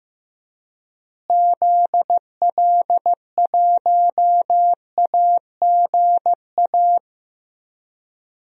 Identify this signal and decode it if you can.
Morse code, heard as a keyed tone, about 15 wpm, 700 Hz. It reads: ZL1AGA
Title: DXCC-100 Call Signs - 6 Characters - Repeated 3x - 15wpm